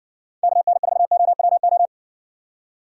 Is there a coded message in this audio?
Yes: Morse code from 0.4 to 1.9 s, heard as a keyed tone, about 56 wpm, 700 Hz. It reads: KN4OWO